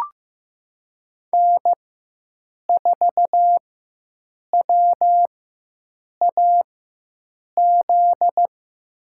Morse code, heard as a keyed tone, about 15 wpm, spaced as for 7 wpm, 700 Hz. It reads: N4WAZ
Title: USA Call Signs - 5 Characters - 4x Character Spacing - 15wpm